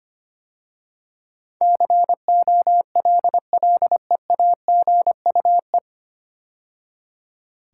Morse code, heard as a keyed tone, about 25 wpm, 690 Hz. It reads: COLLEAGUE
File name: practice / 25wpm / 9 Letter Words - First 3 Letters Given - 25wpm